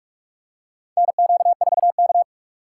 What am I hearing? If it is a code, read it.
Morse code, heard as a keyed tone, about 46 wpm, 700 Hz. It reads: NQ4K
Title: USA Call Signs - 4 Characters - 4x Character Spacing - 45wpm